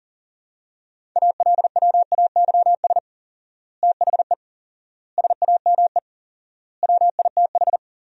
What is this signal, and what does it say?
Morse code, heard as a keyed tone, about 37 wpm, 700 Hz. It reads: ALWAYS THE SAME WITH
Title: 4x Word-Spacing - Sentences from Top 300 Words - 40wpm